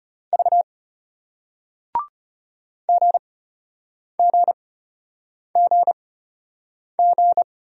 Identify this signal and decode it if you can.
Morse code, heard as a keyed tone, about 36 wpm, spaced as for 8 wpm, 700 Hz. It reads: V GGGG